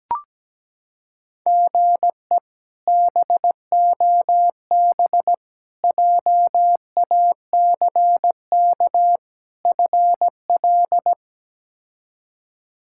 Morse code, heard as a keyed tone, about 17 wpm, 700 Hz. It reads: GE BOB JACK FL
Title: K1USN SST QSO Practice - 17wpm